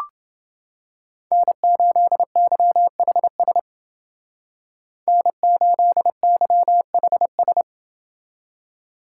Morse code, heard as a keyed tone, about 28 wpm, 700 Hz. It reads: N8YHS N8YHS